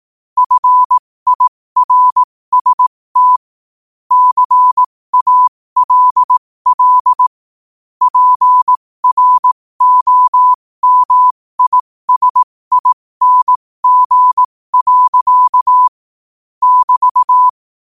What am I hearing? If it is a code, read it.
Morse code, heard as a keyed tone, about 18 wpm, spaced as for 15 wpm, 1000 Hz. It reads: FIRST CALL PROMISING. <BT>
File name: qotdfiles / 15wpm